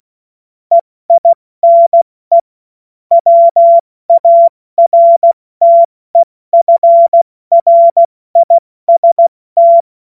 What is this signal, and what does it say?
Morse code, heard as a keyed tone, about 16 wpm, 680 Hz. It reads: EINE WARTEFRIST
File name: Morsekurs / Witze / Witz 25. Oktober 2022 70 F80